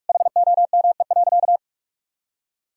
Morse code, heard as a keyed tone, about 46 wpm, 700 Hz. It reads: HOME.
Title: Sentences from Top 1000 Words - Review Part 3 - 45wpm